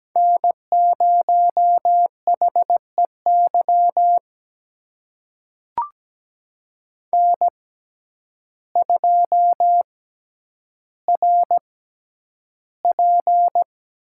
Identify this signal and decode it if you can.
Morse code, heard as a keyed tone, about 17 wpm, spaced as for 9 wpm, 700 Hz. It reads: N0HEY N2RP
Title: USA Call Signs - 5 Characters - 6x Character Spacing - 17wpm